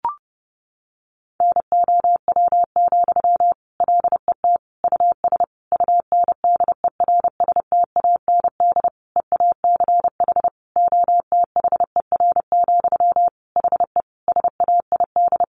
Morse code, heard as a keyed tone, about 30 wpm, 700 Hz. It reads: NOW, LET US UNDERSTAND EACH OTHER, HE SAID
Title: Sentences from Top 1500 Words 30wpm